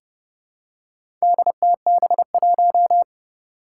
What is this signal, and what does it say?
Morse code, heard as a keyed tone, about 30 wpm, 700 Hz. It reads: DTB1
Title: Four Challenging Letter-Number Combinations 30wpm